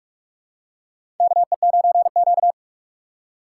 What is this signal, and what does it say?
Morse code, heard as a keyed tone, about 42 wpm, 700 Hz. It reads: KE9Q
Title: USA Call Signs - 4 Characters - 4x Character Spacing - 45wpm